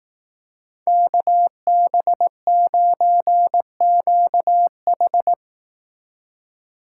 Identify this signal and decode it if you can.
Morse code, heard as a keyed tone, about 18 wpm, 700 Hz. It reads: KB9QH
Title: Speed Racing - USA 2x2 Call Signs 15wpm